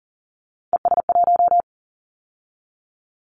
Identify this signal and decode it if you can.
Morse code, heard as a keyed tone, about 40 wpm, 700 Hz. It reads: ES1
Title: Callsign Prefixes for Top 50 DXCC - 40wpm